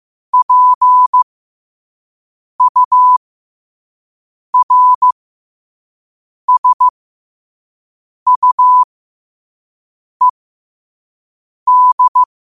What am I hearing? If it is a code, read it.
Morse code, heard as a keyed tone, about 15 wpm, spaced as for 5 wpm, 1000 Hz. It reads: PURSUED
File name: qotdfiles / 5wpm